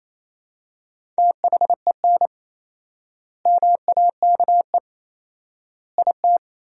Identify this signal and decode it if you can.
Morse code, heard as a keyed tone, about 28 wpm, 700 Hz. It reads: THEN MAKE IT
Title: 4x Word-Spacing - Sentences from Top 100 Words - 28wpm